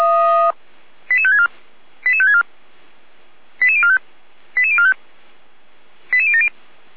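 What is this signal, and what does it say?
Five-tone selective call (ZVEI), heard as quick runs of five tones, tones of about 70 ms. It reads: 88564 88564 88045 88045 88070
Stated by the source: Five-tone ZVEI selective call received by SDR, from audio.f85360.feuerwehr-orlamuende.org, a fire brigade (Feuerwehr Orlamünde) server.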